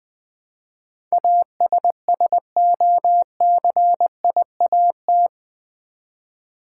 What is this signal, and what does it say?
Morse code, heard as a keyed tone, about 20 wpm, 700 Hz. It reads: ASSOCIAT
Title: Build A Word - 4 Letters At A Time - 10-Letter Words - 20wpm